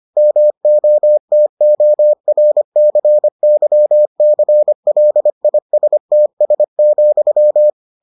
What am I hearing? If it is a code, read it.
Morse code, heard as a keyed tone, about 25 wpm, 600 Hz. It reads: MOTORCYCLISTS,